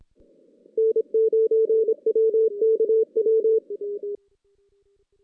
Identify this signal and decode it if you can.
Morse code, heard as a keyed tone, about 28 wpm, 440 Hz. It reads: N9WKW
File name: 2016 JAN CW NAQP / 2016 Jan naqp k5ot - 0-5